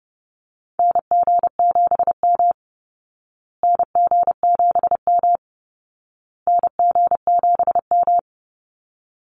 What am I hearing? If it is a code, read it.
Morse code, heard as a keyed tone, about 30 wpm, 700 Hz. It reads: NG7M NG7M NG7M
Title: Call Signs Repeated 2x - 30wpm